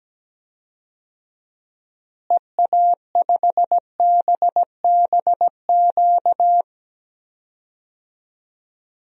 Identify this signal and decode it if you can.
Morse code, heard as a keyed tone, about 17 wpm, 700 Hz. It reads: EA5BBQ
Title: DXCC-100 Call Signs - 6 Characters - Repeated 2x - 17wpm